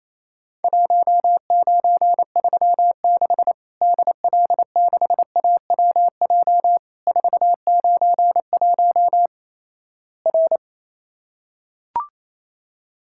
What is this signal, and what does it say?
Morse code, heard as a keyed tone, about 28 wpm, 700 Hz. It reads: 1936 DL6AWJ 491 R
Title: Worked All Europe DX Contest - QTC Exchange - 28wpm